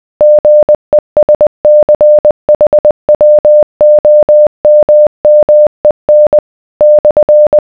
Morse code, heard as a keyed tone, about 20 wpm, 600 Hz. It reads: GESCHWOMMEN /